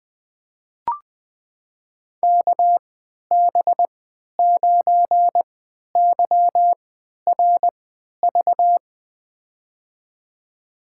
Morse code, heard as a keyed tone, about 20 wpm, spaced as for 11 wpm, 700 Hz. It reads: KB9YRV
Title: USA Call Signs - 3x Character Spacing - 20wpm